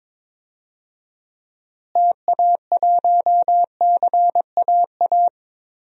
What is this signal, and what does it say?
Morse code, heard as a keyed tone, about 22 wpm, 700 Hz. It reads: TA1CAA
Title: DXCC-100 Call Signs - 6 Characters - Repeated 2x - 22wpm